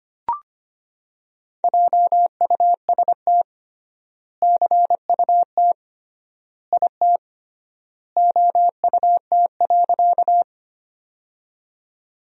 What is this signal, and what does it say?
Morse code, heard as a keyed tone, about 25 wpm, 700 Hz. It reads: JUST CUT IT OUT.
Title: 3x Word-Spacing - Sentences from Top 300 Words - 25wpm